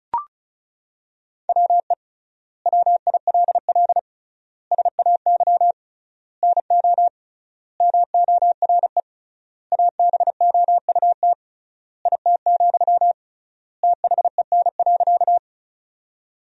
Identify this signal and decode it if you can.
Morse code, heard as a keyed tone, about 34 wpm, 700 Hz. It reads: WE WILL SAY NO MORE ABOUT IT, THEN.
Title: 3x Word-Spacing - Sentences from Top 200 Words - 35wpm